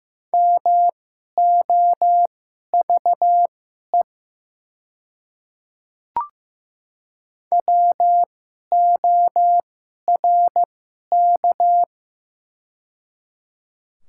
Morse code, heard as a keyed tone, about 15 wpm, spaced as for 11 wpm, 700 Hz. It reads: MOVE WORK
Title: 2x Character Spacing - Top 200 Words - 15wpm